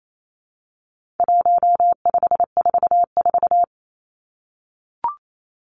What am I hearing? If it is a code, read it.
Morse code, heard as a keyed tone, about 28 wpm, 700 Hz. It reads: 1544
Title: Numbers - 4-Digit 28wpm